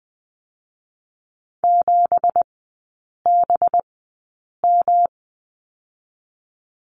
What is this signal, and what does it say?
Morse code, heard as a keyed tone, about 20 wpm, spaced as for 8 wpm, 700 Hz. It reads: 7BM
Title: 184 Lesson 30e - Letter b - Three Characters 20wpm